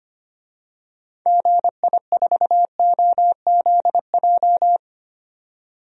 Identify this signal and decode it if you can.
Morse code, heard as a keyed tone, about 25 wpm, 700 Hz. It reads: GI4OZJ
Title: DXCC-100 Call Signs - 6 Characters - 25wpm